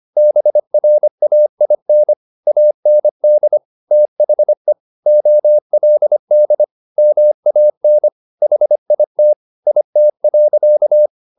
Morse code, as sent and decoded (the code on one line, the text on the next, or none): -... .-. .- .. -. / .- -. -.. / - .... / --- .-.. -.. / -- .- -. / .... .. - / .. - .-.-.-
BRAIN AND THE OLD MAN HIT IT.